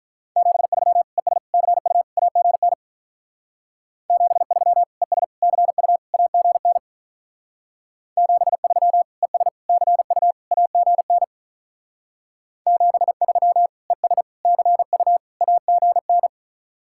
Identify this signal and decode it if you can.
Morse code, heard as a keyed tone, about 46 wpm, 700 Hz. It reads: 73 ES CU AGN 73 ES CU AGN 73 ES CU AGN 73 ES CU AGN